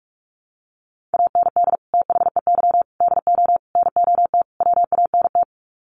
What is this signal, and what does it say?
Morse code, heard as a keyed tone, about 44 wpm, 700 Hz. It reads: AND THEY DO NOT WANT
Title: Sets of 5 Words Encore2 45wpm